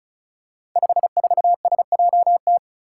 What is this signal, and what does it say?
Morse code, heard as a keyed tone, about 36 wpm, 700 Hz. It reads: 54SJT
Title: Five Challenging Letter-Number Combinations 35wpm